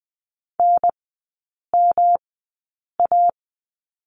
Morse code, heard as a keyed tone, about 20 wpm, spaced as for 8 wpm, 700 Hz. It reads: NMA
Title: 127 Lesson 21e - Number 6 - Three Characters 20wpm